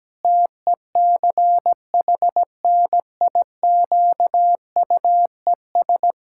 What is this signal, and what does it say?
Morse code, heard as a keyed tone, about 17 wpm, 700 Hz. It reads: TECHNIQUES
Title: Pangram Challenge v2 - 17wpm